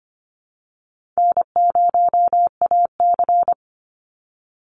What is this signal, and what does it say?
Morse code, heard as a keyed tone, about 25 wpm, 700 Hz. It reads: N0AC
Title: Easy Call Signs Repeated 2x - 25wpm